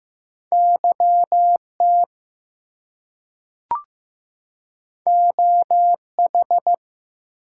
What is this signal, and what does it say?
Morse code, heard as a keyed tone, about 15 wpm, 700 Hz. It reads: YT OH
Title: US States and Canadian Provinces 15wpm